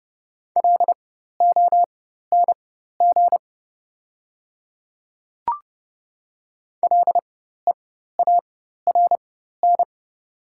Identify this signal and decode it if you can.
Morse code, heard as a keyed tone, about 30 wpm, spaced as for 14 wpm, 700 Hz. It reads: LONG LEARN
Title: 4x Character Spacing - Top 300 Words - 30wpm